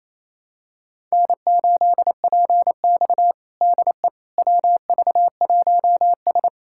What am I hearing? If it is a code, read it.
Morse code, heard as a keyed tone, about 28 wpm, 700 Hz. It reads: N8PX DE WV1S
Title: USA Callsign DE Callsign - 4 Characters - 28wpm